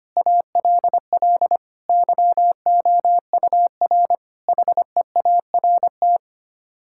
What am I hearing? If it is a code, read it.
Morse code, heard as a keyed tone, about 25 wpm, 700 Hz. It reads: ALL YOUR HEART
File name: Sets of 9 Words - 25wpm